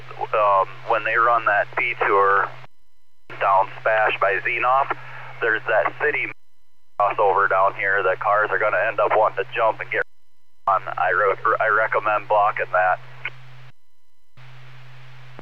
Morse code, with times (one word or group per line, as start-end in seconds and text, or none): none